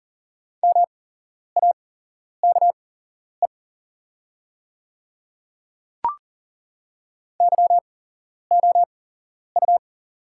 Morse code, heard as a keyed tone, about 40 wpm, spaced as for 11 wpm, 700 Hz. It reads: MAKE YOU
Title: 8x Character Spacing - Top 100 Words - 40wpm